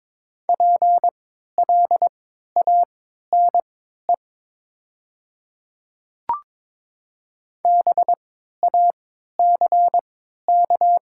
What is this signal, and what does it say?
Morse code, heard as a keyed tone, about 22 wpm, spaced as for 12 wpm, 700 Hz. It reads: PLANE BACK
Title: 3x Character Spacing - Top 500 Words - 22wpm